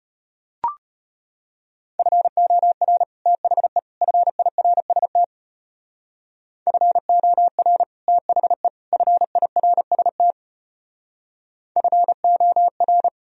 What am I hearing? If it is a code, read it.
Morse code, heard as a keyed tone, about 32 wpm, 700 Hz. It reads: FOR THE FIRST FOR THE FIRST FOR